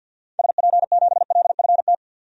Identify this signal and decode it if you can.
Morse code, heard as a keyed tone, about 52 wpm, 700 Hz. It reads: SP7LFT
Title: DXCC-100 Call Signs - 6 Characters - 50wpm